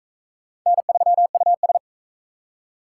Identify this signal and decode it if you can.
Morse code, heard as a keyed tone, about 41 wpm, 700 Hz. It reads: N3US